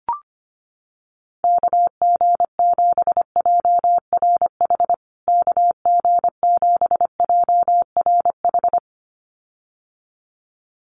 Morse code, heard as a keyed tone, about 25 wpm, 700 Hz. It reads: KG7JRH KG7JRH
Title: USA Call Signs Repeated 2x 25wpm